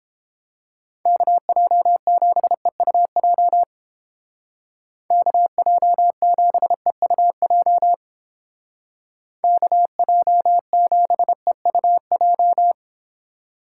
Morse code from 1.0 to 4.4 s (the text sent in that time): KJ7EUJ